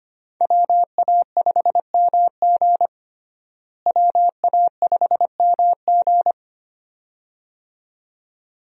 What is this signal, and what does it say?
Morse code, heard as a keyed tone, about 25 wpm, 700 Hz. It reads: WA5MG WA5MG
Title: USA Call Signs - 5 Characters - Repeated 2x - 25wpm